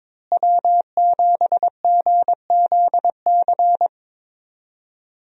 Morse code, heard as a keyed tone, about 21 wpm, 700 Hz. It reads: W7GZC
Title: USA Call Signs - 5 Characters - Repeated 2x - 22wpm